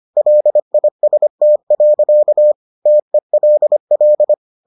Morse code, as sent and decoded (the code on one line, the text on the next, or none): .-.. .. ... - .-.-.- / - . .-.. .-..
LIST. TELL